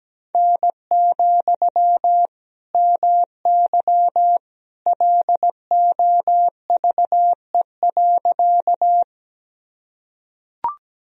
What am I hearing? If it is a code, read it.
Morse code, heard as a keyed tone, about 17 wpm, 700 Hz. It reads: N, MY LOVE.
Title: Sentences from Top 400 Words and Abbreviations 17wpm